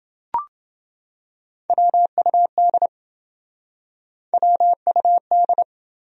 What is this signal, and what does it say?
Morse code, heard as a keyed tone, about 27 wpm, 700 Hz. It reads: WUD WUD